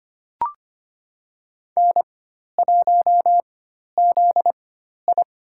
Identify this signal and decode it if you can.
Morse code, heard as a keyed tone, about 25 wpm, spaced as for 12 wpm, 700 Hz. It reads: N1ZI
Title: USA Call Signs - 4 Characters - 4x Character Spacing - 25wpm